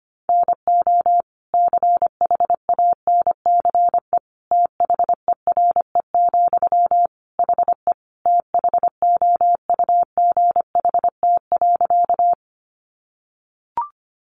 Morse code, heard as a keyed tone, about 25 wpm, 700 Hz. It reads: NO CHANCE THERE, HE THOUGHT.
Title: Sentences from Top 800 Words 25wpm